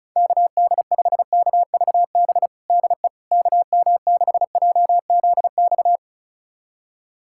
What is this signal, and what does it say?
Morse code, heard as a keyed tone, about 36 wpm, 700 Hz. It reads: KD5KVB DE KM6JZX
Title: USA Callsign DE Callsign - 6 Characters - 35wpm